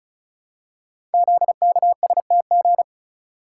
Morse code, heard as a keyed tone, about 36 wpm, 700 Hz. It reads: ZKSTG
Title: Code Groups - 5 Letters - 35wpm